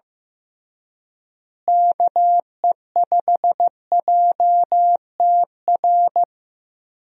Morse code, heard as a keyed tone, about 15 wpm, 700 Hz. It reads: KE5JTR
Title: USA Call Signs - 6 Characters - Repeated 3x - 15wpm